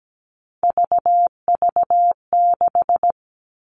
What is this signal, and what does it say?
Morse code, heard as a keyed tone, about 17 wpm, 700 Hz. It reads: VV6
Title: All Cut Numbers - 3-Digit 17wpm